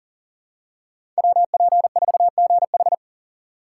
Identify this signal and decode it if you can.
Morse code, heard as a keyed tone, about 40 wpm, 700 Hz. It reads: WP4GH
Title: USA Call Signs - 2x Character Spacing - 40wpm